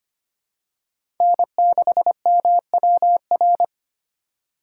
Morse code, heard as a keyed tone, about 25 wpm, 700 Hz. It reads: N6MWR
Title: USA Call Signs - 5 Characters - 4x Character Spacing - 25wpm